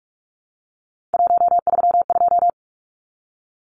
Morse code, heard as a keyed tone, about 44 wpm, 700 Hz. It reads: 132
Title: Numbers - 3-Digit 45wpm